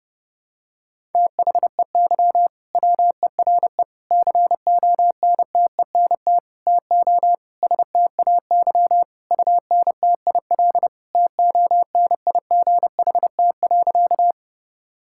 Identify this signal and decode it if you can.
Morse code, heard as a keyed tone, about 30 wpm, 700 Hz. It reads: THEY WERE CONTENT TO STAY UNTIL TONIGHT.
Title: Sentences from Top 1000 Words - Review Part 6 - 30wpm